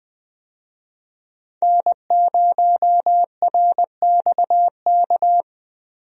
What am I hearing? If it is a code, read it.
Morse code, heard as a keyed tone, about 20 wpm, 700 Hz. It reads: N0RXK
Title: USA Call Signs - 5 Characters - 20wpm